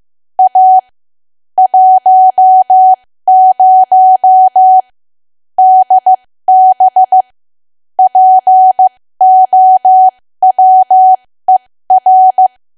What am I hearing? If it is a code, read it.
Morse code, heard as a keyed tone, about 15 wpm, spaced as for 13 wpm, 750 Hz. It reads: A 10 DB POWER
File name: cwa / W1AW-Level2 / 13WPM